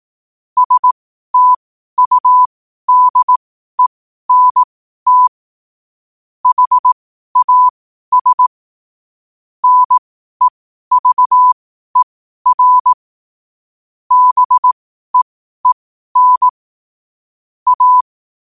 Morse code, heard as a keyed tone, about 18 wpm, spaced as for 12 wpm, 1000 Hz. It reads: STUDENT HAS NEVER BEEN A
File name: qotdfiles / 12wpm